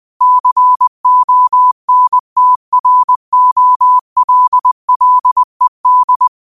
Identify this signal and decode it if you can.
Morse code, heard as a keyed tone, about 20 wpm, 1000 Hz. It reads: CONTROLLED